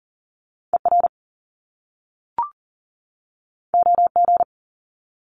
Morse code, heard as a keyed tone, about 42 wpm, 700 Hz. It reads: ER OG
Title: Top Two Letter Combinations in Words 40wpm